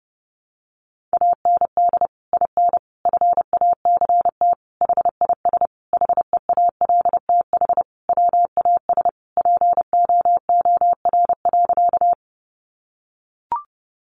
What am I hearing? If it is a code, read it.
Morse code, heard as a keyed tone, about 30 wpm, 700 Hz. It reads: AND IN FACT HIS HEALTH WAS POOR.
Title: Sentences from Top 1000 Words 30wpm